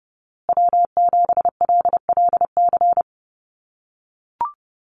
Morse code, heard as a keyed tone, about 30 wpm, 700 Hz. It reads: W7LLC
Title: USA Call Signs Repeated 2x 30wpm